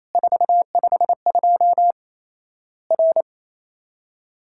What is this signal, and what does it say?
Morse code, heard as a keyed tone, about 28 wpm, 700 Hz. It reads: 452 R
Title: Worked All Europe DX Contest - QTC Exchange - 28wpm